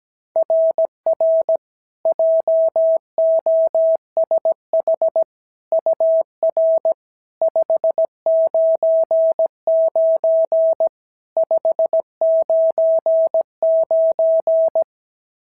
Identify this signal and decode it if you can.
Morse code, heard as a keyed tone, about 16 wpm, 650 Hz. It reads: RR JOSH UR 599 599
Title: SOTA QSO Practice 17wpm